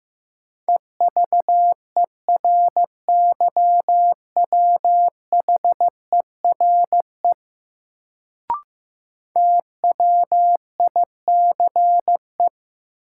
Morse code, heard as a keyed tone, about 15 wpm, 700 Hz. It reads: EVERYWHERE TWICE
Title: Sentences from Top 1600 Words - New Word Drill Alt - 15wpm